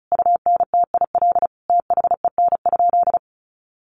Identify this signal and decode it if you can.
Morse code, heard as a keyed tone, about 34 wpm, 700 Hz. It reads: UNTIL THEN?
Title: Sentences from Top 500 Words 35wpm